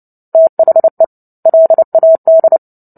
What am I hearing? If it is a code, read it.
Morse code, heard as a keyed tone, about 30 wpm, 650 Hz. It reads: THE LAD
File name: Book Two / 03-Chapter 3 - 29 WPM